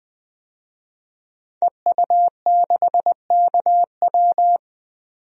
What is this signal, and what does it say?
Morse code, heard as a keyed tone, about 20 wpm, 700 Hz. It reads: EU6KW
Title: DXCC-100 Call Signs - 5 Characters - 20wpm